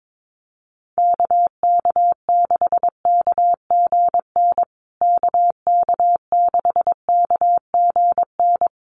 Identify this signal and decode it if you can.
Morse code, heard as a keyed tone, about 22 wpm, 700 Hz. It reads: KK6KGN KK6KGN
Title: USA Call Signs Repeated 3x 22wpm